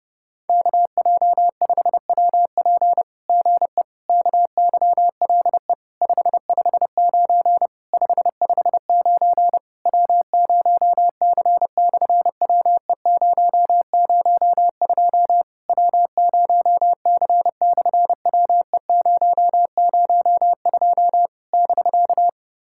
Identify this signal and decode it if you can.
Morse code, heard as a keyed tone, about 30 wpm, 700 Hz. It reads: KJ5WP GE KYLE 559 559 W0C/WE002 W0C/WE002 <BK>
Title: SOTA QSO Practice 30wpm